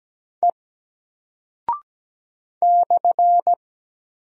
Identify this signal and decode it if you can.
Morse code, heard as a keyed tone, about 17 wpm, 700 Hz. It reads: E /
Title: Single Character (Letter:Number:Punctuation:Prosign) 17wpm